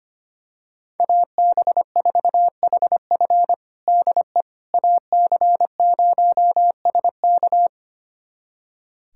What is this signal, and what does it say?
Morse code, heard as a keyed tone, about 25 wpm, 700 Hz. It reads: AB4HF DE AC0SK
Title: USA Callsign DE Callsign - 5 Characters - 25wpm